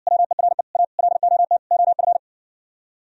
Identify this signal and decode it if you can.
Morse code, heard as a keyed tone, about 59 wpm, 700 Hz. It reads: WERE A LOT OF